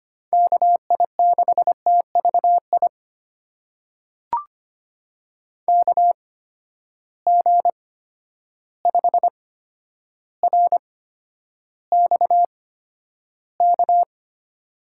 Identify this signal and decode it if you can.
Morse code, heard as a keyed tone, about 25 wpm, spaced as for 11 wpm, 700 Hz. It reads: KI6TVI KG5RXK
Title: USA Call Signs - 6 Characters - 8x Character Spacing - 25wpm